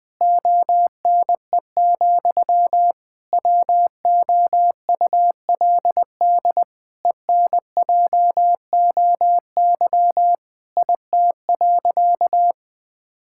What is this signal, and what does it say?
Morse code, heard as a keyed tone, about 21 wpm, 700 Hz. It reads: ONE, WOULD ENJOY IT.
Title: Sentences from Top 1000 Words - Review Part 5 - 20wpm